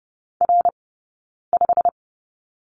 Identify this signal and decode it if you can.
Morse code, heard as a keyed tone, about 30 wpm, spaced as for 9 wpm, 690 Hz. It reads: R5
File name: Lesson 19d - Letter w - Two Characters 30wpm